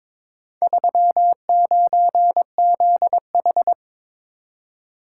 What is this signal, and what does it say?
Morse code, heard as a keyed tone, about 22 wpm, 700 Hz. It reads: 39ZH
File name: practice / 22wpm / Four Letter-Number Combinations - 22wpm